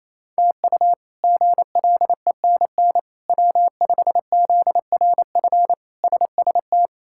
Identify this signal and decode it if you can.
Morse code, heard as a keyed tone, about 28 wpm, 700 Hz. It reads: TU GLENN W5ZRF SST
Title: K1USN SST QSO Practice - 28wpm